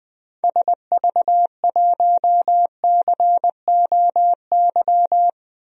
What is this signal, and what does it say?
Morse code, heard as a keyed tone, about 20 wpm, 700 Hz. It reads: SV1COY
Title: DXCC-100 Call Signs - 6 Characters - 20wpm